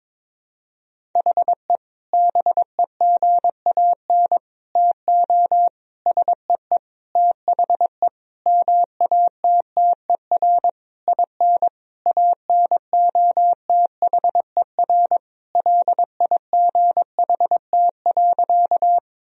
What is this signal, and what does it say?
Morse code, heard as a keyed tone, about 22 wpm, 700 Hz. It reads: HE BEGAN TO SEE THE MATTER IN ANOTHER LIGHT.
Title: Sentences from Top 1000 Words - Review Part 1 - 22wpm